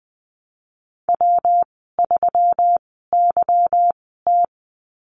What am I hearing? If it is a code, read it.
Morse code, heard as a keyed tone, about 20 wpm, spaced as for 15 wpm, 700 Hz. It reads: W3YT
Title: 186 Lesson 30g - Letter b - Call Signs 20wpm